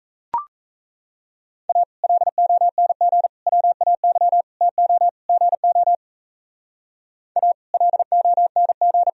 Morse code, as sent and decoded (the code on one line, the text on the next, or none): .- / .-.. --- -. --. / .-- .- -.-- / - --- / --. --- / .- / .-.. --- -. --.
A LONG WAY TO GO A LONG